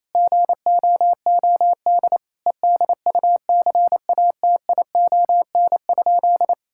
Morse code, heard as a keyed tone, about 27 wpm, 700 Hz. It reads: GOOD EDUCATION?
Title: Sentences from Top 1000 Words - Review Part 6 - 28wpm